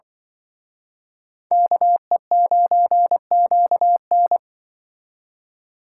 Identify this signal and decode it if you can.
Morse code, heard as a keyed tone, about 24 wpm, 700 Hz. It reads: KE9QN
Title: Speed Racing - USA Call Signs 20wpm